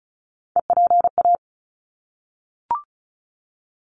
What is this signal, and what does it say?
Morse code, heard as a keyed tone, about 34 wpm, 700 Hz. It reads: EPA
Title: ARRL Sections 35wpm